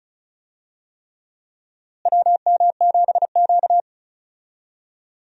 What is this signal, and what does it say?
Morse code, heard as a keyed tone, about 36 wpm, 700 Hz. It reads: WM7Q
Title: USA Call Signs - 4 Characters - Repeated 3x - 35wpm